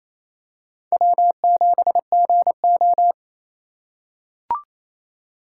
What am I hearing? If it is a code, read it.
Morse code, heard as a keyed tone, about 28 wpm, 700 Hz. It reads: W7GO
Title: USA Call Signs - 3x Character Spacing - 28wpm